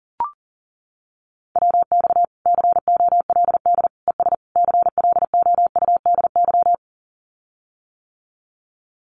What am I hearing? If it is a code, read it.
Morse code, heard as a keyed tone, about 40 wpm, 700 Hz. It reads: WX COLD ES CLOUDY
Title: Common QSO Phrases Encore 40wpm